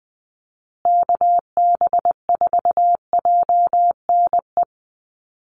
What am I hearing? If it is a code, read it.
Morse code, heard as a keyed tone, about 20 wpm, 700 Hz. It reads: KB4JNE
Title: USA Call Signs Repeated 3x 20wpm